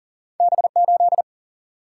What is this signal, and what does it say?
Morse code, heard as a keyed tone, about 40 wpm, 700 Hz. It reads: B 8
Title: Single-Letter Single-Number 40wpm